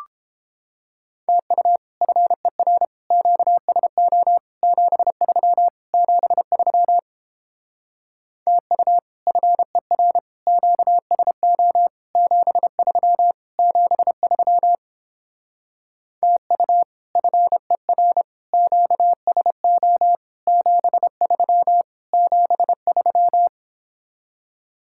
Morse code, heard as a keyed tone, about 31 wpm, 700 Hz. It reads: TU FER QSO 73 73 TU FER QSO 73 73 TU FER QSO 73 73